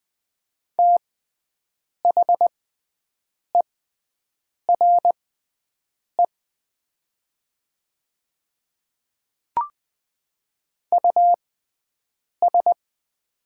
Morse code, heard as a keyed tone, about 20 wpm, spaced as for 7 wpm, 700 Hz. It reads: THERE US
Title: 6x Character Spacing - Top 200 Words - 20wpm